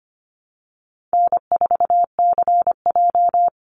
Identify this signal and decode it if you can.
Morse code, heard as a keyed tone, about 25 wpm, 700 Hz. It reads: N4CJ
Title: Easy Call Signs - 25wpm